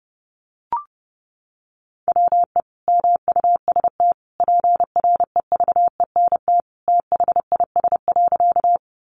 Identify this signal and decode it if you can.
Morse code, heard as a keyed tone, about 30 wpm, 700 Hz. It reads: WE MUST PREVENT THIS.
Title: Sentences from Top 1100 Words 30wpm